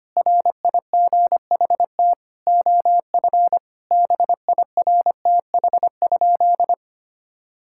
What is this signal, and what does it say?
Morse code, heard as a keyed tone, about 25 wpm, 700 Hz. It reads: RIGHT OF BIRTH?
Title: Sentences from Top 1000 Words - Review Part 6 - 25wpm